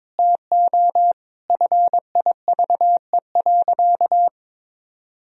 Morse code, heard as a keyed tone, about 21 wpm, 700 Hz. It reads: TO FIVE.
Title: Sentences from Top 1000 Words - Review Part 2 - 22wpm